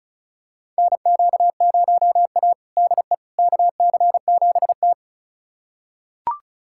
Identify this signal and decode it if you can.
Morse code, heard as a keyed tone, about 36 wpm, 700 Hz. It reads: NQ0A DE KC7T
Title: USA Callsign DE Callsign - 4 Characters - 35wpm